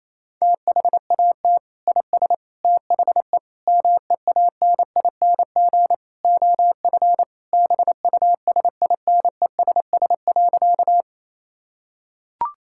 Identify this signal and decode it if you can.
Morse code, heard as a keyed tone, about 28 wpm, 700 Hz. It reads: THAT IS THE MEANING OF BUSINESS.
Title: Sentences from Top 1000 Words - Review Part 6 - 28wpm